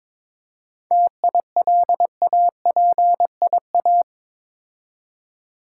Morse code, heard as a keyed tone, about 22 wpm, 700 Hz. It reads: TILAPIA
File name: Difficult Words with Hints - 22wpm